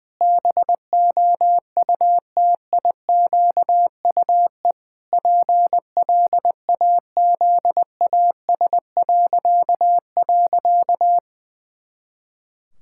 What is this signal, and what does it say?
Morse code, heard as a keyed tone, about 20 wpm, 700 Hz. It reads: BOUTIQUE PLAZAS..
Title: Pangram Challenge v2 - 20wpm